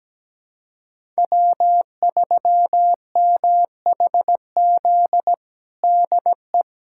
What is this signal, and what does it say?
Morse code, heard as a keyed tone, about 17 wpm, 700 Hz. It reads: W3MHZ DE
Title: USA Callsign DE Callsign - 5 Characters - 17wpm